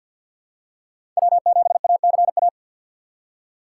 Morse code, heard as a keyed tone, about 47 wpm, 700 Hz. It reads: W7ACA
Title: USA Call Signs - 5 Characters - 6x Character Spacing - 50wpm